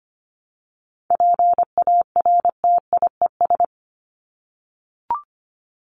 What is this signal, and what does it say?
Morse code, heard as a keyed tone, about 25 wpm, 700 Hz. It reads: PARTIES
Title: ES:S Suffix - Top 100 Words with Base Words 25wpm